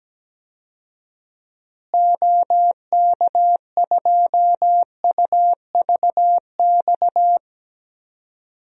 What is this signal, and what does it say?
Morse code, heard as a keyed tone, about 17 wpm, 700 Hz. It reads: OK2UVX
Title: DXCC-100 Call Signs - 6 Characters - Repeated 3x - 17wpm